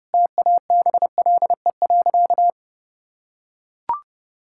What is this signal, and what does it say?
Morse code, heard as a keyed tone, about 32 wpm, 700 Hz. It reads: TABLE.
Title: Sentences from Top 1000 Words - Review Part 6 - 30wpm